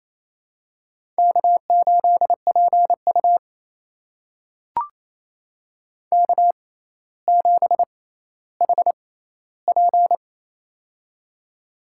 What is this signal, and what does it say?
Morse code, heard as a keyed tone, about 28 wpm, spaced as for 14 wpm, 700 Hz. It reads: K8PU K7HP
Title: USA Call Signs - 4 Characters - 6x Character Spacing - 28wpm